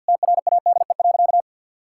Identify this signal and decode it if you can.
Morse code, heard as a keyed tone, about 48 wpm, 700 Hz. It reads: TRADE.
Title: Sentences from Top 1600 Words 50wpm